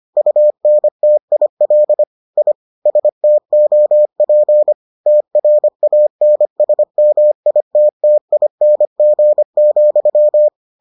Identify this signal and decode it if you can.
Morse code, heard as a keyed tone, about 25 wpm, 600 Hz. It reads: UNTIL I STOP TRANSMITTING,